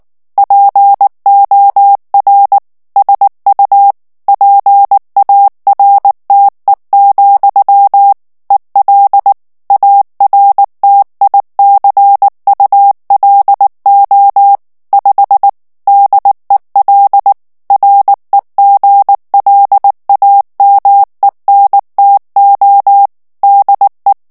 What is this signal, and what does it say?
Morse code, heard as a keyed tone, about 19 wpm, 800 Hz. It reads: POR SU PARTE, EL ARTICULO 5 DEL REGLAMENTO DE
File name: morse mp3 / ESPAGNOL TEXTE EN CLAIR 2 VITESSE 18 MOTS MINUTES